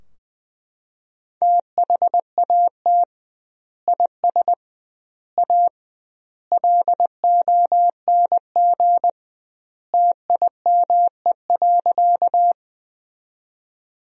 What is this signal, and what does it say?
Morse code, heard as a keyed tone, about 20 wpm, 700 Hz. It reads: THAT IS A LONG TIME.
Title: 2x Word-Spacing - Sentences from Top 100 Words - 20wpm